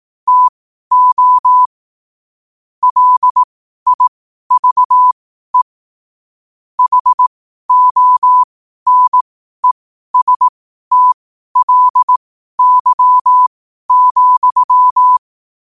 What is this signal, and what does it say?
Morse code, heard as a keyed tone, about 18 wpm, spaced as for 12 wpm, 1000 Hz. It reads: TO LIVE HONESTLY,
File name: qotdfiles / 12wpm